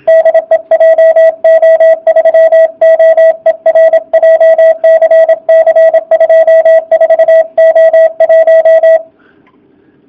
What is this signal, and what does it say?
Morse code, heard as a keyed tone, about 26 wpm, 660 Hz, with radio noise. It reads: DE JO3OER JCC 24O1
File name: jh8cbh / 2019cw / sp32- w